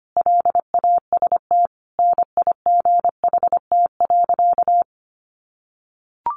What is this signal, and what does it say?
Morse code, heard as a keyed tone, about 25 wpm, 700 Hz. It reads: LAST NIGHT.
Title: Sentences from Top 300 Words 25wpm